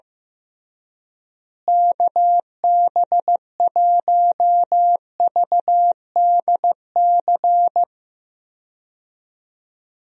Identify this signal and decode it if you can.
Morse code, heard as a keyed tone, about 14 wpm, 700 Hz. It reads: KB1VDC
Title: USA Call Signs - 6 Characters - Repeated 3x - 15wpm